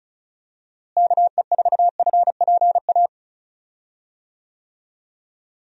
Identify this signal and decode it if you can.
Morse code, heard as a keyed tone, about 36 wpm, 700 Hz. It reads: KE4FPA
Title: USA Call Signs - 6 Characters - Repeated 3x - 35wpm